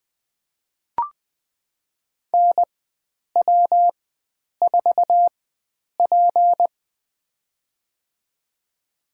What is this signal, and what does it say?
Morse code, heard as a keyed tone, about 20 wpm, spaced as for 9 wpm, 700 Hz. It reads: NW4P
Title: USA Call Signs - 4 Characters - 4x Character Spacing - 20wpm